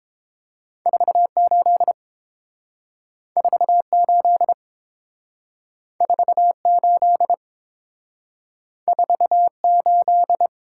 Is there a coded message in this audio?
Yes: Morse code from 0.9 to 10.5 s, heard as a keyed tone, about 30 wpm, 700 Hz. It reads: 48 48 48 48